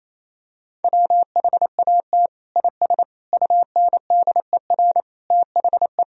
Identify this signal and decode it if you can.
Morse code, heard as a keyed tone, about 28 wpm, 700 Hz. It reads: WHAT IS UNDER THE
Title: Sentences from Top 1000 Words - Review Part 2 - 28wpm